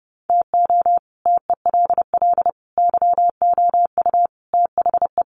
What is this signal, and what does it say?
Morse code, heard as a keyed tone, about 30 wpm, 700 Hz. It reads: TO TELL YOU THE
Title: Sets of 5 Words 30wpm